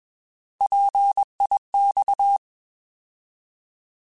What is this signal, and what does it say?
Morse code, heard as a keyed tone, about 21 wpm, 800 Hz. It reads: PIX